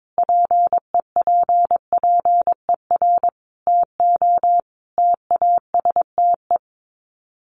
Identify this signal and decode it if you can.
Morse code, heard as a keyed tone, about 22 wpm, 700 Hz. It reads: PEPPER TO TASTE
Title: Sets of 5 Words 22wpm